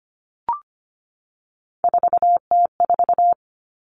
Morse code, heard as a keyed tone, about 25 wpm, 700 Hz. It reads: 4T4
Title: Common Cut Numbers - 3-Digit 25wpm